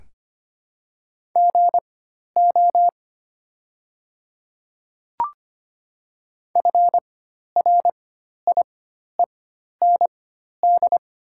Morse code, heard as a keyed tone, about 25 wpm, spaced as for 12 wpm, 700 Hz. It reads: GO FRIEND
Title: 4x Character Spacing - Top 300 Words - 25wpm